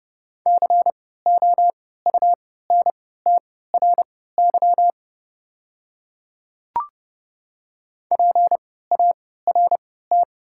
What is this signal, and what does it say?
Morse code, heard as a keyed tone, about 30 wpm, spaced as for 17 wpm, 690 Hz. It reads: COUNTRY PART
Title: 3x Character Spacing - Top 300 Words - 30wpm